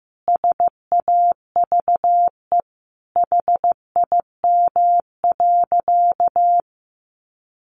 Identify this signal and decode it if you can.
Morse code, heard as a keyed tone, about 15 wpm, 700 Hz. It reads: SAVE HIM.
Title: Sentences from Top 1100 Words 15wpm